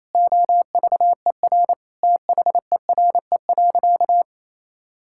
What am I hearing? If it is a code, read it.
Morse code, heard as a keyed tone, about 28 wpm, 700 Hz. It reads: OVER THERE.
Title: Sentences from Top 1600 Words 28wpm